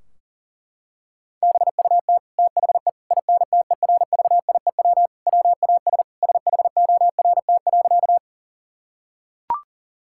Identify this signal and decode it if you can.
Morse code, heard as a keyed tone, about 40 wpm, 700 Hz. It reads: BUT THE INTERVIEW WAS SHORT.
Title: Sentences from Top 1600 Words 40wpm